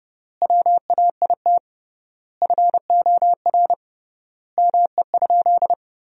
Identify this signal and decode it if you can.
Morse code, heard as a keyed tone, about 32 wpm, 700 Hz. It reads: WAIT FOR ME?
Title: 3x Word-Spacing - Sentences from Top 500 Words - 30wpm